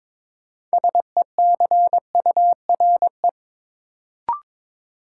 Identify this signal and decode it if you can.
Morse code, heard as a keyed tone, about 22 wpm, 700 Hz. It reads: SECURE
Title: Sentences from Top 1200 Words - New Word Drill Alt - 22wpm